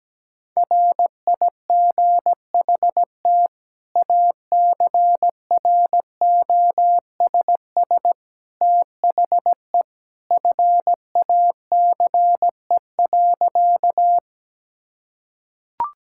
Morse code, heard as a keyed tone, about 17 wpm, 700 Hz. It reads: RIGHT ACROSS THE FACE.
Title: Sentences from Top 1000 Words - Review Part 5 - 17wpm